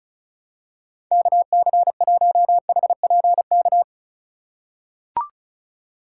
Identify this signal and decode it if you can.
Morse code, heard as a keyed tone, about 36 wpm, 700 Hz. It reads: KC1HPK
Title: USA Call Signs - 6 Characters - 2x Character Spacing - 35wpm